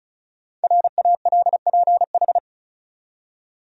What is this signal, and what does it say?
Morse code, heard as a keyed tone, about 34 wpm, 700 Hz. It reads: RALPH
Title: USA Common Names - 35wpm